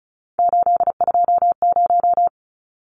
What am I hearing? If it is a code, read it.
Morse code, heard as a keyed tone, about 34 wpm, 700 Hz. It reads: 820
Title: Numbers - 3-Digit 35wpm